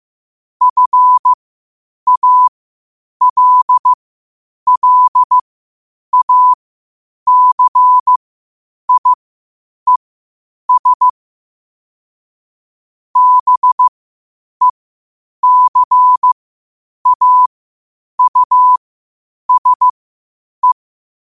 Morse code, heard as a keyed tone, about 15 wpm, spaced as for 8 wpm, 1000 Hz. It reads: FALLACIES BECAUSE
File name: qotdfiles / 8wpm